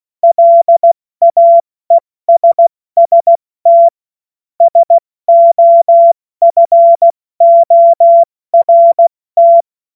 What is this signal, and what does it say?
Morse code, heard as a keyed tone, about 16 wpm, spaced as for 14 wpm, 680 Hz. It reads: LAESST SOFORT
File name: Morsekurs/Witze